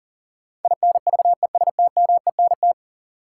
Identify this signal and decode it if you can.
Morse code, heard as a keyed tone, about 40 wpm, 700 Hz. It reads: INVESTMENT
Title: Sentences from Top 700 Words - New Word Drill - 40wpm